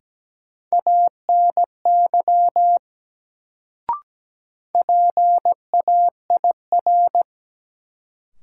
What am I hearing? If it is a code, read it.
Morse code, heard as a keyed tone, about 17 wpm, 700 Hz. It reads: ANY PAIR
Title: Sets of 3 Words - New Word Drill - 17wpm